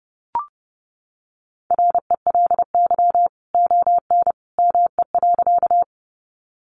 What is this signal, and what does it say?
Morse code, heard as a keyed tone, about 30 wpm, 700 Hz. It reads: RELY ON ME.
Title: Sentences from Top 1200 Words 30wpm